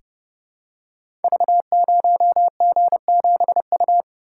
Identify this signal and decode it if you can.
Morse code, heard as a keyed tone, about 28 wpm, 700 Hz. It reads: V0G7U
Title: Code Groups - 5 Letter:Numbers - 30wpm